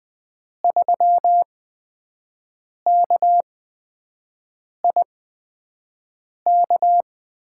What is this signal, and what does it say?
Morse code, heard as a keyed tone, about 20 wpm, spaced as for 5 wpm, 700 Hz. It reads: 3KIK
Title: USA Call Signs - 8x Character Spacing - 20wpm